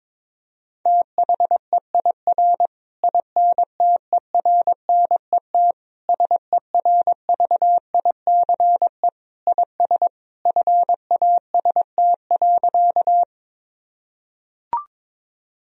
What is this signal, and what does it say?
Morse code, heard as a keyed tone, about 21 wpm, 700 Hz. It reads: THEIR INTERNET SERVICE IS FAST.
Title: Sentences from Top 1000 Words - Review Part 4 - 22wpm